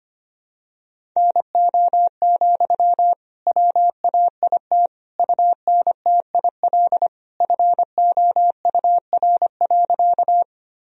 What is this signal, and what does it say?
Morse code, heard as a keyed tone, about 25 wpm, 700 Hz. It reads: NO, WAIT UNTIL FOUR.
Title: Sentences from Top 1000 Words - Review Part 1 - 25wpm